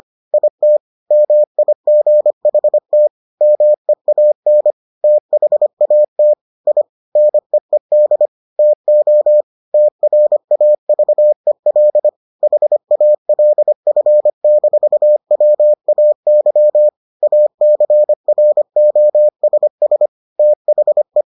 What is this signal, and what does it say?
Morse code, heard as a keyed tone, about 25 wpm, 600 Hz. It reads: IT MIGHT MEAN THAT I NEED TO TRAVEL HALF-WAY ACROSS THE